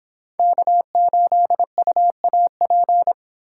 Morse code, heard as a keyed tone, about 25 wpm, 700 Hz. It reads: K8UAP